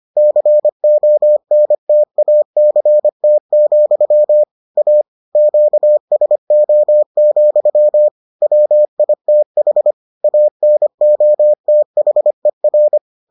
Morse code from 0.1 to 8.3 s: CONTACT, A QSO,

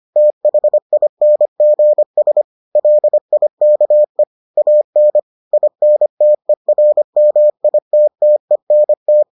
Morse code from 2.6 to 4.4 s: LIKE

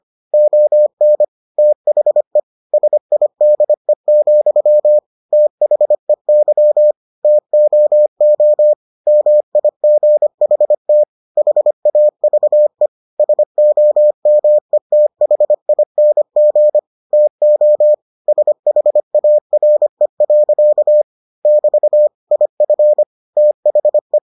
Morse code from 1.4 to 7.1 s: THE SIDE, THEY

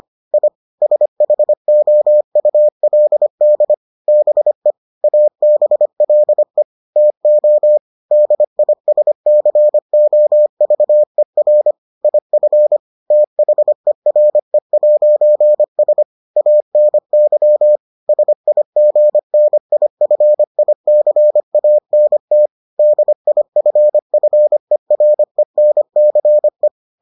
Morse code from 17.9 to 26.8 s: SIGNIFICANT DIFFERENCE

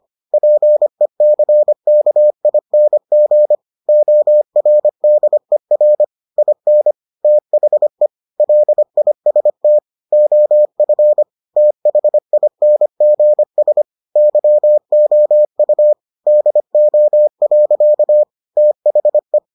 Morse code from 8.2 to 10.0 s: LIST

